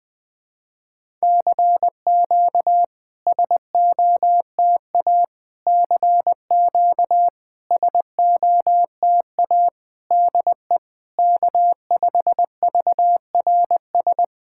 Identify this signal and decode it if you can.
Morse code, heard as a keyed tone, about 20 wpm, 700 Hz. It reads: CQ SOTA CQ SOTA DE K5VRS
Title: SOTA QSO Practice 20wpm